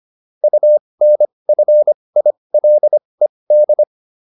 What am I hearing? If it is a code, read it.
Morse code, heard as a keyed tone, about 25 wpm, spaced as for 20 wpm, 600 Hz. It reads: UNFILED